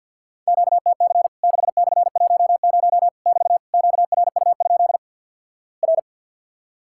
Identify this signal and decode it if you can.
Morse code, heard as a keyed tone, about 50 wpm, 700 Hz. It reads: QTC 6/10 <BT> QRU? R
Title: Worked All Europe DX Contest - QTC Exchange with Repeats - 50wpm